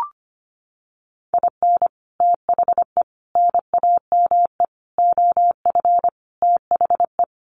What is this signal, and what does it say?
Morse code, heard as a keyed tone, about 25 wpm, 700 Hz. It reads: IN THE NAME OF THE
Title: Sets of 5 Words Encore 25wpm